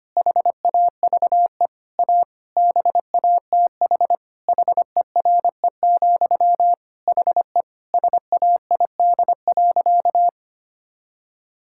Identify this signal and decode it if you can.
Morse code, heard as a keyed tone, about 25 wpm, 700 Hz. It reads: HAVE A BATH HERE, HE SAID.
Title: Sentences from Top 1600 Words 25wpm